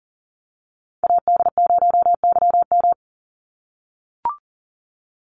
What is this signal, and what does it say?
Morse code, heard as a keyed tone, about 39 wpm, 700 Hz. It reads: AD0YM
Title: NA SOTA Chaser Callsigns 40wpm